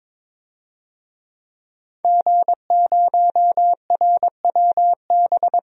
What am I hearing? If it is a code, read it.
Morse code, heard as a keyed tone, about 22 wpm, 700 Hz. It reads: G0RWB
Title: DXCC-100 Call Signs - 5 Characters - 22wpm